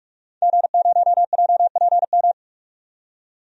Morse code, heard as a keyed tone, about 42 wpm, 700 Hz. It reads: G0JPM
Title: DXCC-100 Call Signs - 5 Characters - 45wpm